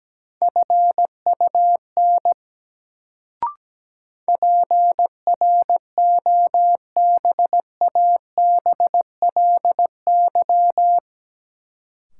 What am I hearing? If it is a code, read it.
Morse code, heard as a keyed tone, about 17 wpm, 700 Hz. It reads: FUN PROBABLY
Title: Sentences from Top 700 Words - New Word Drill - 17wpm